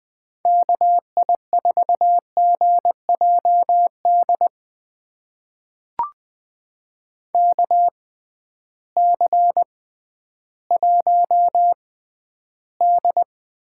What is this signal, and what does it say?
Morse code, heard as a keyed tone, about 20 wpm, spaced as for 12 wpm, 700 Hz. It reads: KI4GJD KC1D
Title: USA Call Signs - 6 Characters - 6x Character Spacing - 20wpm